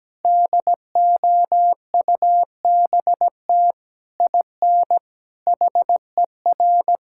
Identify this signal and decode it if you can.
Morse code, heard as a keyed tone, about 17 wpm, 700 Hz. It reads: DOUBT IN HER
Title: Sentences from Top 1600 Words 17wpm